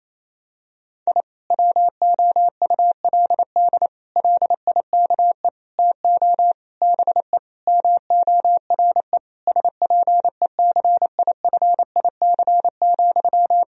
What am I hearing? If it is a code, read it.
Morse code, heard as a keyed tone, about 28 wpm, 700 Hz. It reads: I WOULD LIKE TO BE MORE SPECIFIC,
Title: Sentences from Top 1000 Words - Review Part 3 - 28wpm